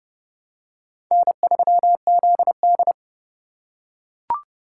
Morse code, heard as a keyed tone, about 32 wpm, 700 Hz. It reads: N3ZD
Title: USA Call Signs - 4 Characters - 3x Character Spacing - 30wpm